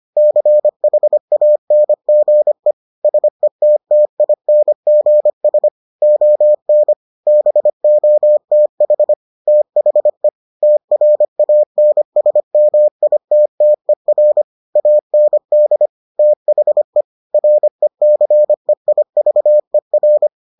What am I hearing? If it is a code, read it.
Morse code, heard as a keyed tone, about 25 wpm, 600 Hz. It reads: CHANGE SETTINGS ON BOTH THE TRANSMITTER AND THE RECEIVER